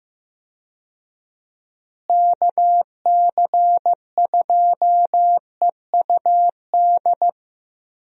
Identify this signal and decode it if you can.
Morse code, heard as a keyed tone, about 15 wpm, 700 Hz. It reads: KC2EUD
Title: USA Call Signs - 6 Characters - Repeated 2x - 15wpm